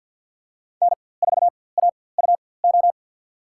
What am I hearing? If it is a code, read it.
Morse code, heard as a keyed tone, about 50 wpm, spaced as for 23 wpm, 690 Hz. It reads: N4AUO
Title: USA Call Signs - 5 Characters - 4x Character Spacing - 50wpm